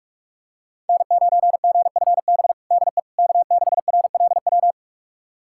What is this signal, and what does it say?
Morse code, heard as a keyed tone, about 43 wpm, 700 Hz. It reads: N9GFB DE K6RLW
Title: USA Callsign DE Callsign - 5 Characters - 45wpm